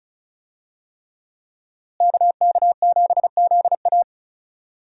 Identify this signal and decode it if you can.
Morse code, heard as a keyed tone, about 36 wpm, 700 Hz. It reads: KK7ZA
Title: USA Call Signs - 5 Characters - Repeated 3x - 35wpm